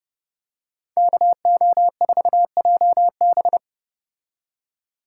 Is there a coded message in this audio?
Yes: Morse code from 1.0 to 3.6 s, heard as a keyed tone, about 31 wpm, 700 Hz. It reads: KO4JB